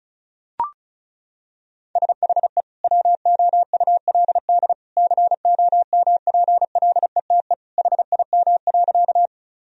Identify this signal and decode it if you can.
Morse code, heard as a keyed tone, about 36 wpm, 700 Hz. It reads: SHE WOULD COMPLETE HIM.
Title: Sentences from Top 1000 Words - Review Part 2 - 35wpm